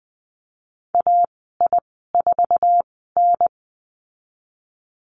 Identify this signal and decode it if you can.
Morse code, heard as a keyed tone, about 20 wpm, spaced as for 15 wpm, 700 Hz. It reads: AI4N
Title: 062 Lesson 11g - Letter h - Call Signs 20wpm